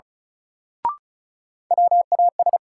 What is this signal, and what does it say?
Morse code, heard as a keyed tone, about 34 wpm, 700 Hz. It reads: WAS
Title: Sentences from Top 100 Words - New Word Drill - 35wpm